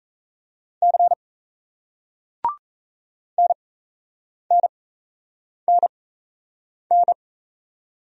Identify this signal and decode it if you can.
Morse code, heard as a keyed tone, about 38 wpm, spaced as for 8 wpm, 700 Hz. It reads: C NNNN